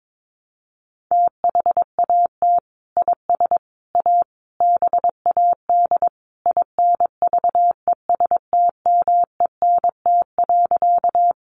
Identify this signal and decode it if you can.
Morse code, heard as a keyed tone, about 22 wpm, 700 Hz. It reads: THAT IS A BAD INVESTMENT.
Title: Sentences from Top 700 Words 22wpm